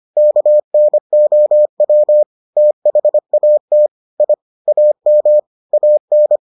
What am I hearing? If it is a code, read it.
Morse code, heard as a keyed tone, about 25 wpm, 600 Hz. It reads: KNOW THAT I AM AN